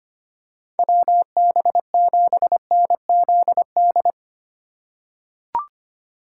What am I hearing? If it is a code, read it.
Morse code, heard as a keyed tone, about 25 wpm, 700 Hz. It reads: WB7NZD
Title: USA Call Signs - 6 Characters - 2x Character Spacing - 25wpm